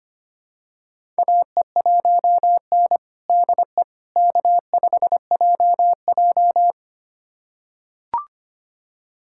Morse code, heard as a keyed tone, about 25 wpm, 700 Hz. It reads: AE1N DE K5JJ
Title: USA Callsign DE Callsign - 4 Characters - 25wpm